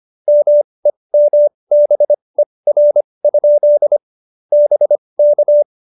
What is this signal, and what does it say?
Morse code, heard as a keyed tone, about 25 wpm, spaced as for 20 wpm, 590 Hz. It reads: MEMBER? BK
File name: QSO316 20